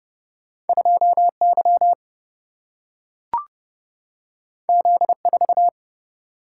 Morse code, heard as a keyed tone, about 31 wpm, 700 Hz. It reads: 2Y Z4
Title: Single-Letter Single-Number 30wpm